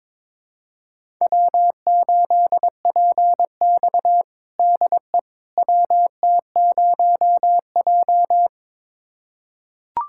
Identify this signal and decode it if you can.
Morse code, heard as a keyed tone, about 22 wpm, 700 Hz. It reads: W8PX DE WT0J
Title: USA Callsign DE Callsign - 4 Characters - 22wpm